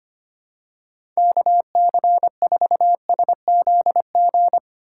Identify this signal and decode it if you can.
Morse code, heard as a keyed tone, about 25 wpm, 700 Hz. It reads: KC4SZG
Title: USA Call Signs - 6 Characters - 2x Character Spacing - 25wpm